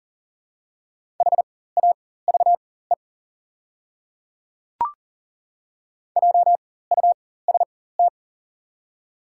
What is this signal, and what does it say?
Morse code, heard as a keyed tone, about 40 wpm, spaced as for 19 wpm, 700 Hz. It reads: HAVE JUST
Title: 4x Character Spacing - Top 200 Words - 40wpm